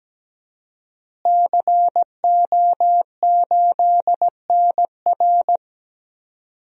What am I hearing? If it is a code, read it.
Morse code, heard as a keyed tone, about 17 wpm, 700 Hz. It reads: CO8NR
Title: DXCC-100 Call Signs - 5 Characters - Repeated 3x - 17wpm